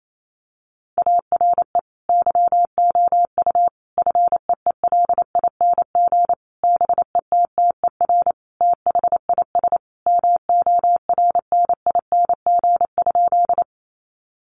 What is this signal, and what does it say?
Morse code, heard as a keyed tone, about 28 wpm, 700 Hz. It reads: ARE YOU FEELING BETTER THIS MORNING?
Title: Sentences from Top 1200 Words 28wpm